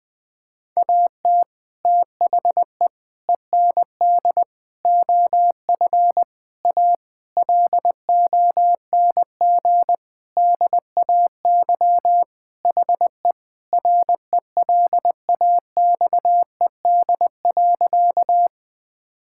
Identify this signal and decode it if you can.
Morse code, heard as a keyed tone, about 21 wpm, 700 Hz. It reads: AT THE END OF A LONG DAY HE RELAXED.
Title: Sentences with 5-Word Phrases - Spoken Beginning - 20wpm